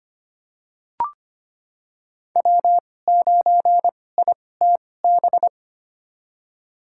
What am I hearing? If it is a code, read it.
Morse code, heard as a keyed tone, about 25 wpm, spaced as for 18 wpm, 700 Hz. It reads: W9ITB
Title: USA Call Signs - 2x Character Spacing - 25wpm